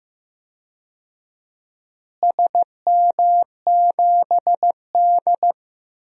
Morse code, heard as a keyed tone, about 15 wpm, 700 Hz. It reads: SM7D
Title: DXCC-100 Call Signs - 4 Characters - Repeated 2x - 15wpm